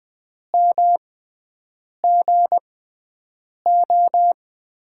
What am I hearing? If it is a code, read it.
Morse code, heard as a keyed tone, about 20 wpm, spaced as for 7 wpm, 700 Hz. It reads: MGO
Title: USA Call Signs - 5 Characters - 6x Character Spacing - 20wpm